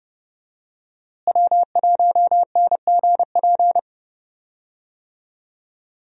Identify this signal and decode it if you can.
Morse code, heard as a keyed tone, about 29 wpm, 700 Hz. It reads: W1NGP
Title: USA Call Signs - 5 Characters - Repeated 3x - 30wpm